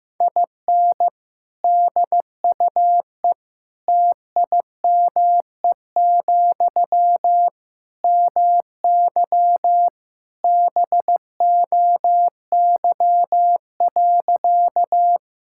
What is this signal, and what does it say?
Morse code, heard as a keyed tone, about 15 wpm, 700 Hz. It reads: IN DUE TIME, MY BOY.
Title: Sentences from Top 1000 Words - Review Part 6 - 15wpm